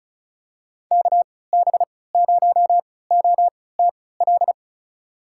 Morse code, heard as a keyed tone, about 34 wpm, spaced as for 20 wpm, 700 Hz. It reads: KB0OTL
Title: USA Call Signs - 6 Characters - 3x Character Spacing - 35wpm